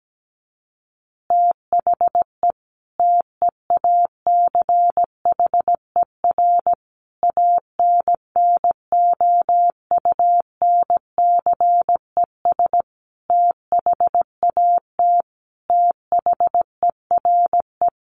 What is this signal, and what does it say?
Morse code, heard as a keyed tone, about 17 wpm, 700 Hz. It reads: THE TEACHER ANNOUNCES THAT THERE
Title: Additional Easy Phrases 17wpm